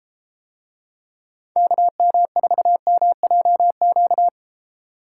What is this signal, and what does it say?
Morse code, heard as a keyed tone, about 33 wpm, 700 Hz. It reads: KM4MJQ